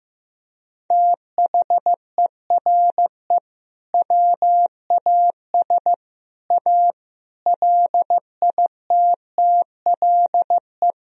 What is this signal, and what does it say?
Morse code, heard as a keyed tone, about 15 wpm, 700 Hz. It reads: THERE WAS A LITTLE
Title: Sentences from Top 1000 Words - Review Part 4 - 15wpm